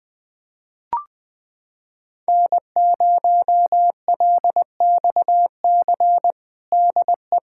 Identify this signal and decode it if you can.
Morse code, heard as a keyed tone, about 20 wpm, 700 Hz. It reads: N0LXC DE
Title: USA Callsign DE Callsign - 5 Characters - 20wpm